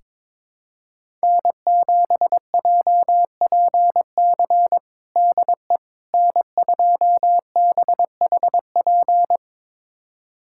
Morse code, heard as a keyed tone, about 22 wpm, 700 Hz. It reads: N7JPC DE N2BHP
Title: USA Callsign DE Callsign - 5 Characters - 22wpm